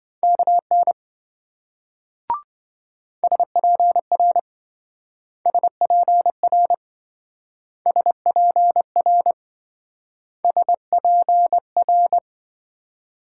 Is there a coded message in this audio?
Yes: Morse code from 0.2 to 12.2 s, heard as a keyed tone, about 26 wpm, 700 Hz. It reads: KN SPR SPR SPR SPR